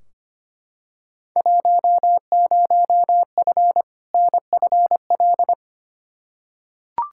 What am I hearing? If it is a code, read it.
Morse code, heard as a keyed tone, about 25 wpm, 700 Hz. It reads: 10F NFL
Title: Field Day Exchanges Encore - 25wpm